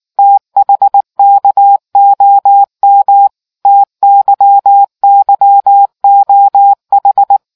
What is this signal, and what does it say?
Morse code, heard as a keyed tone, about 19 wpm, 800 Hz. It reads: THKOM TYYOH